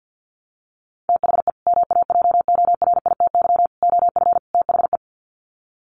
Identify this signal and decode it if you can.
Morse code, heard as a keyed tone, about 50 wpm, 700 Hz. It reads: THE MAJORITY OF THE
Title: Sets of 4 Words 50wpm